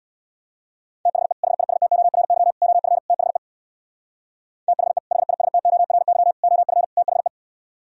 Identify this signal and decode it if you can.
Morse code, heard as a keyed tone, about 64 wpm, 700 Hz. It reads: THE HISTORY OF THE THE HISTORY OF THE